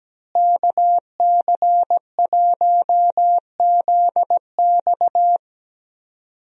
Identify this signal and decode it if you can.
Morse code, heard as a keyed tone, about 17 wpm, 700 Hz. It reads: KC1ZX
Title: USA Call Signs - 5 Characters - Repeated 2x - 17wpm